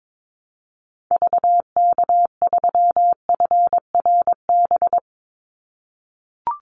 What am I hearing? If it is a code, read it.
Morse code, heard as a keyed tone, about 22 wpm, 700 Hz. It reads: VK3FRB
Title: USA Call Signs 22wpm